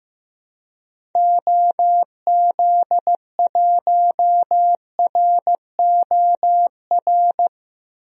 Morse code, heard as a keyed tone, about 15 wpm, 700 Hz. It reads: OZ1ROR
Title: DXCC-100 Call Signs - 6 Characters - 15wpm